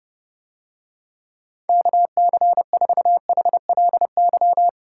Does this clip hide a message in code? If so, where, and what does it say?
Yes: Morse code from 1.7 to 4.7 s, heard as a keyed tone, about 30 wpm, 700 Hz. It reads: KC4HLY